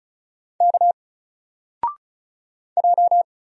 Morse code, heard as a keyed tone, about 36 wpm, 700 Hz. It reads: K J
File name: Single-Letter-Number-35wpm